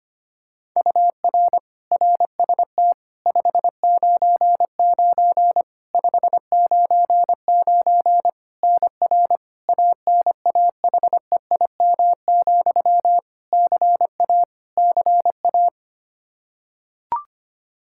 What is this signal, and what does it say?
Morse code, heard as a keyed tone, about 25 wpm, 700 Hz. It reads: UR RST 599 599 NR ANAHEIM, CA CA
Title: Practice QSO - Exchange 1 - Beginning - 25wpm